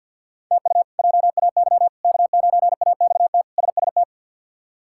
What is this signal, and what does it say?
Morse code, heard as a keyed tone, about 52 wpm, 700 Hz. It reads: TU JAY K9AXT SST
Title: K1USN SST QSO Practice - 50wpm